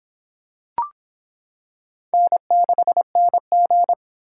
Morse code, heard as a keyed tone, about 24 wpm, 700 Hz. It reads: N6NG